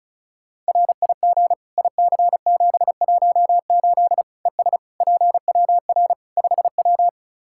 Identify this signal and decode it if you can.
Morse code, heard as a keyed tone, about 34 wpm, 700 Hz. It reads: RIG IC718 ES PWR 5W
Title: Practice QSO - Exchange 2 - Info - 35wpm